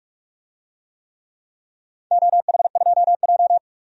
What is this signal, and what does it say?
Morse code, heard as a keyed tone, about 47 wpm, 700 Hz. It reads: OH2J
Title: DXCC-100 Call Signs - 4 Characters - 45wpm